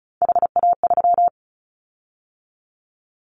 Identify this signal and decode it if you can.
Morse code, heard as a keyed tone, about 36 wpm, 700 Hz. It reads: HA3
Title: Callsign Prefixes for Top 50 DXCC - 35wpm